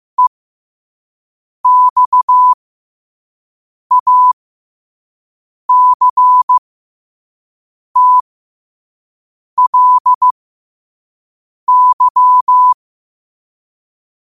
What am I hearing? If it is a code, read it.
Morse code, heard as a keyed tone, about 15 wpm, spaced as for 5 wpm, 1000 Hz. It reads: EXACTLY
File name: qotdfiles / 5wpm